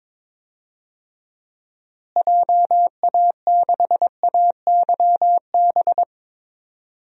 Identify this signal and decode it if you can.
Morse code, heard as a keyed tone, about 21 wpm, 700 Hz. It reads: JA6AYB
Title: DXCC-100 Call Signs - 6 Characters - Repeated 2x - 22wpm